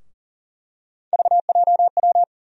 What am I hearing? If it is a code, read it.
Morse code, heard as a keyed tone, about 40 wpm, 700 Hz. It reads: VJW
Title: USA Call Sign Postfixes - 3 Letters 40wpm